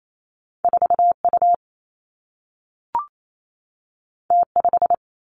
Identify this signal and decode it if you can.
Morse code, heard as a keyed tone, about 28 wpm, 700 Hz. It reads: 4 U T 5
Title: Two Challenging Letter-Number Combinations 28wpm